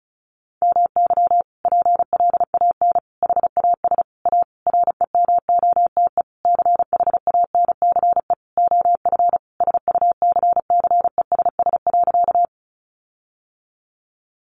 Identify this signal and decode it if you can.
Morse code, heard as a keyed tone, about 34 wpm, 700 Hz. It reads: MY PLAN HAS A REMOTE CHANCE OF SUCCESS.
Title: Sentences from Top 1300 Words 35wpm